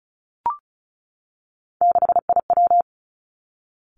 Morse code, heard as a keyed tone, about 36 wpm, 700 Hz. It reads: 6IW
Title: Three Challenging Letter-Number Combinations 35wpm